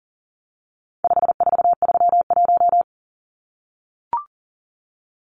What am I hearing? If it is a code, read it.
Morse code, heard as a keyed tone, about 40 wpm, 700 Hz. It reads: 5431
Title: Numbers - 4-Digit 40wpm